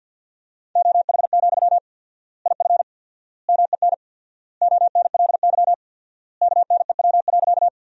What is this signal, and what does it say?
Morse code, heard as a keyed tone, about 51 wpm, 700 Hz. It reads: OH, IF MEN ONLY KNEW.
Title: 4x Word-Spacing - Sentences from Top 500 Words - 50wpm